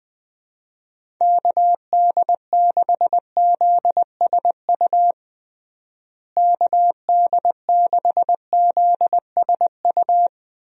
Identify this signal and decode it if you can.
Morse code, heard as a keyed tone, about 20 wpm, 700 Hz. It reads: KD6ZSU KD6ZSU
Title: USA Call Signs - 6 Characters - Repeated 2x - 20wpm